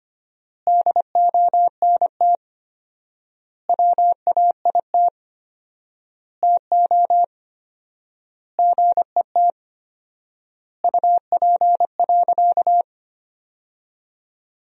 Morse code, heard as a keyed tone, about 25 wpm, 700 Hz. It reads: DONT WAIT TO GET UP.
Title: 4x Word-Spacing - Sentences from Top 500 Words - 25wpm